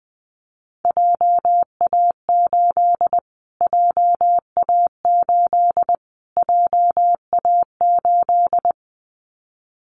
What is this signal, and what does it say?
Morse code, heard as a keyed tone, about 20 wpm, 700 Hz. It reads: JA8 JA8 JA8
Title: Callsign Prefixes for Top 25 DXCC - Repeated 3x - 20wpm